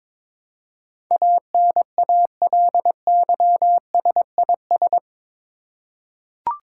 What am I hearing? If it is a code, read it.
Morse code, heard as a keyed tone, about 22 wpm, 700 Hz. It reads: ANALYSIS
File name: practice / 22wpm / Eight Letter Words - 22wpm